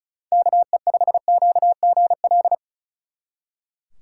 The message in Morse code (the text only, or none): KE5QGL